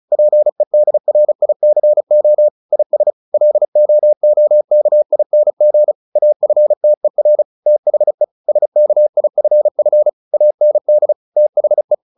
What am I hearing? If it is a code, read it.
Morse code, heard as a keyed tone, about 34 wpm, 600 Hz. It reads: PEDRICO IS LOOKING AFTER THE SKIFF AND THE
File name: the old man and the sea 2 35wpm